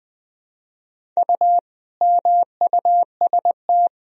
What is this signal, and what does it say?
Morse code, heard as a keyed tone, about 20 wpm, 700 Hz. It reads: U MUST
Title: Sentences from Top 200 Words and Abbreviations 20wpm